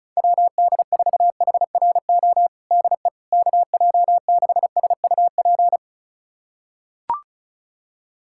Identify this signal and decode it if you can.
Morse code, heard as a keyed tone, about 36 wpm, 700 Hz. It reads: WD4HRO DE KJ6SUP
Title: USA Callsign DE Callsign 35wpm